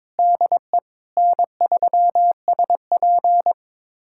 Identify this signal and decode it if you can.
Morse code, heard as a keyed tone, about 22 wpm, 700 Hz. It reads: DE N3SP
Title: USA Callsign DE Callsign - 4 Characters - 22wpm